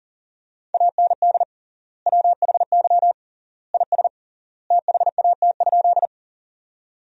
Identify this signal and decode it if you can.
Morse code, heard as a keyed tone, about 40 wpm, 700 Hz. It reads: AND WHY IS THAT?
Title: 3x Word-Spacing - Sentences from Top 200 Words - 40wpm